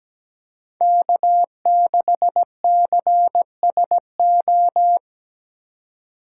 Morse code, heard as a keyed tone, about 20 wpm, 700 Hz. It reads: K6CSO
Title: Speed Racing - USA 1x3 Call Signs 17wpm